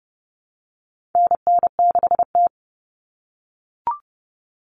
Morse code, heard as a keyed tone, about 30 wpm, 700 Hz. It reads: NN6T
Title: Call Signs - 30wpm